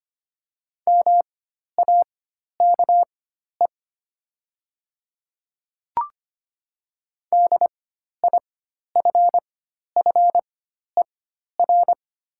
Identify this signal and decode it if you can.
Morse code, heard as a keyed tone, about 25 wpm, spaced as for 12 wpm, 700 Hz. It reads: MAKE DIFFER
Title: 4x Character Spacing - Top 300 Words - 25wpm